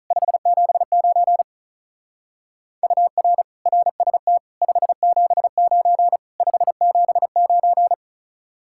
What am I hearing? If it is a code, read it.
Morse code, heard as a keyed tone, about 44 wpm, 700 Hz. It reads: 579 UR RST 579 579